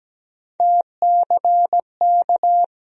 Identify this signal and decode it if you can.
Morse code, heard as a keyed tone, about 17 wpm, 700 Hz. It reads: TCK
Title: USA Call Sign Postfixes - 3 Letters 17wpm